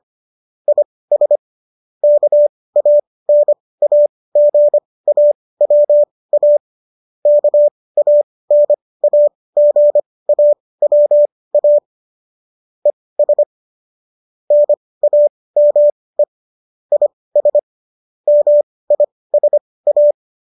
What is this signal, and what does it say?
Morse code, heard as a keyed tone, about 25 wpm, spaced as for 18 wpm, 590 Hz. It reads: IS KANAGAWA KANAGAWA ES NAME IS MISA